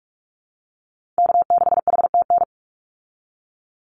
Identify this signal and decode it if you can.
Morse code, heard as a keyed tone, about 44 wpm, 700 Hz. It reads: K6HTN
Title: Call Signs Repeated 3x - 45wpm